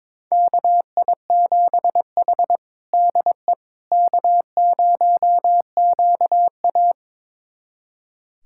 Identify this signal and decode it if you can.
Morse code, heard as a keyed tone, about 22 wpm, 700 Hz. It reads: KI7H DE K0QA
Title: USA Callsign DE Callsign - 4 Characters - 22wpm